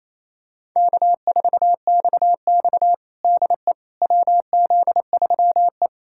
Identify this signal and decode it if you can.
Morse code, heard as a keyed tone, about 28 wpm, 700 Hz. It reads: K4XX DE WZ3E
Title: USA Callsign DE Callsign - 4 Characters - 28wpm